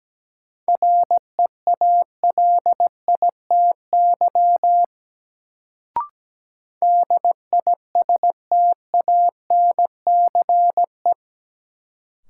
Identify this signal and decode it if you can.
Morse code, heard as a keyed tone, about 17 wpm, 700 Hz. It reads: REALITY DISTANCE
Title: Sentences from Top 1200 Words - New Word Drill - 17wpm